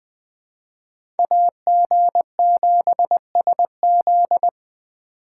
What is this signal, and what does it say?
Morse code, heard as a keyed tone, about 20 wpm, 700 Hz. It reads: AG7SZ
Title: USA Call Signs - 5 Characters - 2x Character Spacing - 20wpm